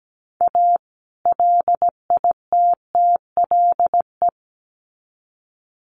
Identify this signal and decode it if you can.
Morse code, heard as a keyed tone, about 17 wpm, 700 Hz. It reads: A LITTLE
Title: Sets of 2 Words 17wpm